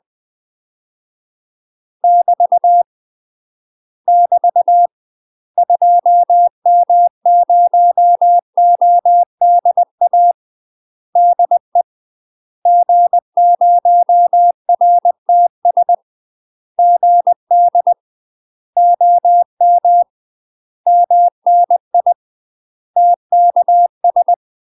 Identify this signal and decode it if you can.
Morse code, heard as a keyed tone, about 20 wpm, 700 Hz. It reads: <BT> <BT> 2M0ODA DE G0RTS GD OM MNI TKS
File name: UK MASTER QSOs TRAINING / 20WPM / UK MASTER QSOs 0003